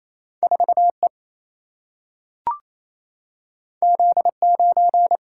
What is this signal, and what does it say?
Morse code, heard as a keyed tone, about 28 wpm, 700 Hz. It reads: 4E Z9
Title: Single-Letter Single-Number 28wpm